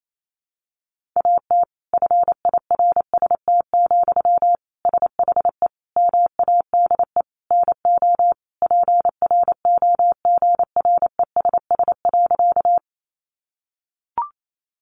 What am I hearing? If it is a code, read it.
Morse code, heard as a keyed tone, about 28 wpm, 700 Hz. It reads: AT FIRST, SHE MADE NO PROGRESS.
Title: Sentences from Top 1400 Words 28wpm